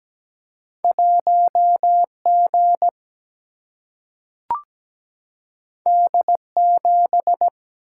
Morse code, heard as a keyed tone, about 17 wpm, 700 Hz. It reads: 1G D7
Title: Single-Letter Single-Number 17wpm